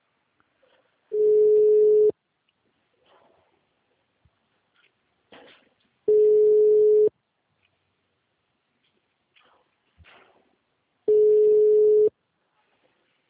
Ringback tone (Europe, 425 Hz).
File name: out- -2012- - - .36768